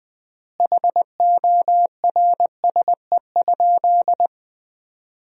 Morse code, heard as a keyed tone, about 20 wpm, 700 Hz. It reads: HORSE?
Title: 3x Word-Spacing - Sentences from Top 500 Words - 20wpm